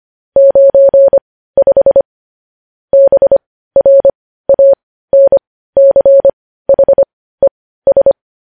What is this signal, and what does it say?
Morse code, heard as a keyed tone, about 25 wpm, spaced as for 15 wpm, 550 Hz. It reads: 95 BRANCHES
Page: news headlines in morse code at 15 wpm